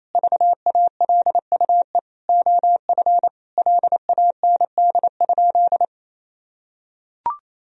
Morse code, heard as a keyed tone, about 28 wpm, 700 Hz. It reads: VALUE OF LAND?
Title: Sentences from Top 1000 Words - Review Part 3 - 28wpm